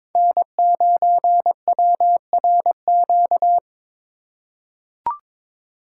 Morse code, heard as a keyed tone, about 22 wpm, 700 Hz. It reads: N9WRQ
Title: USA Call Signs - 5 Characters - 2x Character Spacing - 22wpm